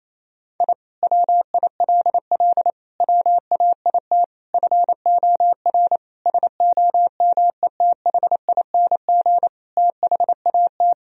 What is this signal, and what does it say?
Morse code, heard as a keyed tone, about 28 wpm, 700 Hz. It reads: I WILL WAIT FOR SOMETHING THAT
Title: Sentences from Top 1000 Words - Review Part 1 - 28wpm